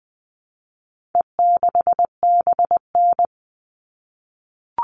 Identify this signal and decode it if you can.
Morse code, heard as a keyed tone, about 20 wpm, 700 Hz. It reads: E6BN
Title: All Cut Numbers - 4-Digit 20wpm